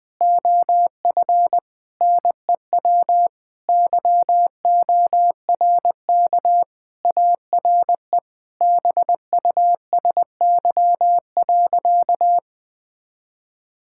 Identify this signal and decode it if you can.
Morse code, heard as a keyed tone, about 21 wpm, 700 Hz. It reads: OF NEW YORK ARE BUSY.
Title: Sentences with 3-Word Phrases - Spoken Beginning - 20wpm